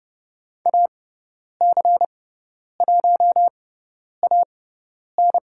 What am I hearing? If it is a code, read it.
Morse code, heard as a keyed tone, about 30 wpm, spaced as for 10 wpm, 700 Hz. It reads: AC1AN
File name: Lesson 18g - Letter m - Call Signs 30wpm